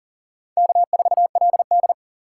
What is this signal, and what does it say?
Morse code, heard as a keyed tone, about 39 wpm, 700 Hz. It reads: K4LD
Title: USA Call Signs - 4 Characters - 2x Character Spacing - 40wpm